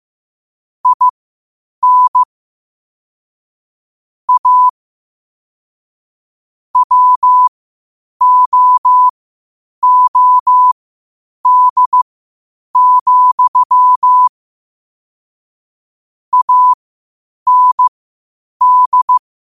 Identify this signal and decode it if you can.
Morse code, heard as a keyed tone, about 15 wpm, spaced as for 8 wpm, 1000 Hz. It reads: IN A WOOD, AND
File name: qotdfiles / 8wpm